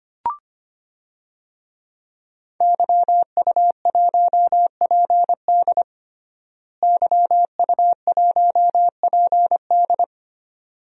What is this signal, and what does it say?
Morse code, heard as a keyed tone, about 25 wpm, 700 Hz. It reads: YU1PD YU1PD
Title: DXCC-100 Call Signs - 5 Characters - Repeated 2x - 25wpm